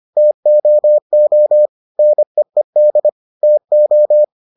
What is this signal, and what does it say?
Morse code, heard as a keyed tone, about 25 wpm, 600 Hz. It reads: TOO NEED TO